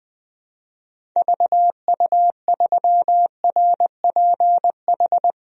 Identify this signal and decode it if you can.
Morse code, heard as a keyed tone, about 20 wpm, 700 Hz. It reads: VU3RPH
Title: DXCC-100 Call Signs - 6 Characters - 20wpm